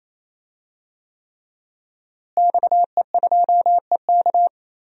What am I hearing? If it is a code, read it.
Morse code, heard as a keyed tone, about 28 wpm, 700 Hz. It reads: XE2EK
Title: DXCC-100 Call Signs - 5 Characters - Repeated 3x - 28wpm